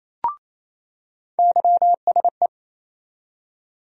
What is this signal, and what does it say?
Morse code, heard as a keyed tone, about 28 wpm, 700 Hz. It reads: YSE
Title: USA Call Sign Postfixes - 3 Letters 28wpm